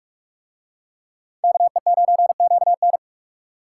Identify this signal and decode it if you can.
Morse code, heard as a keyed tone, about 43 wpm, 700 Hz. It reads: KE9QN